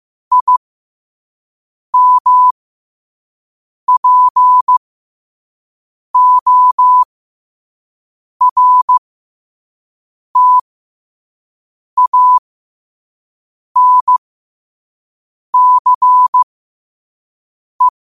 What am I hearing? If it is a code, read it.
Morse code, heard as a keyed tone, about 15 wpm, spaced as for 5 wpm, 1000 Hz. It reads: IMPORTANCE
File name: qotdfiles / 5wpm